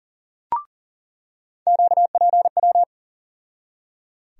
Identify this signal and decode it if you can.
Morse code, heard as a keyed tone, about 38 wpm, 700 Hz. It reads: QPW
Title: USA Call Sign Postfixes - 3 Letters 40wpm